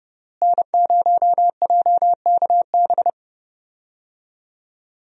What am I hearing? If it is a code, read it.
Morse code, heard as a keyed tone, about 32 wpm, 700 Hz. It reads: N0JKB
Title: USA Call Signs - 5 Characters - Repeated 2x - 30wpm